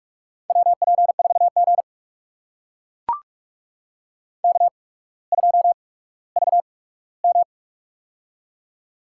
Morse code, heard as a keyed tone, about 44 wpm, spaced as for 19 wpm, 700 Hz. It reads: WP4G K2VM
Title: USA Call Signs - 4 Characters - 8x Character Spacing - 45wpm